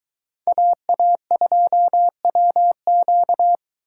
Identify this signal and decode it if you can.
Morse code, heard as a keyed tone, about 22 wpm, 700 Hz. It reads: AA2WQ